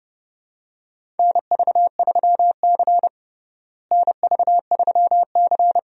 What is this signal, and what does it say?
Morse code, heard as a keyed tone, about 30 wpm, 700 Hz. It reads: NV3C NV3C
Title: USA Call Signs - 4 Characters - Repeated 2x - 30wpm